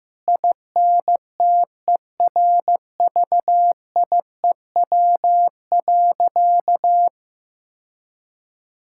Morse code, heard as a keyed tone, about 15 wpm, 700 Hz. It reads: INTERVIEW.
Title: Sentences from Top 1600 Words 15wpm